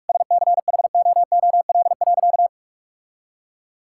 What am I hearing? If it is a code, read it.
Morse code, heard as a keyed tone, about 41 wpm, 700 Hz. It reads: SCHOOL.
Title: Sentences from Top 1000 Words - Review Part 6 - 45wpm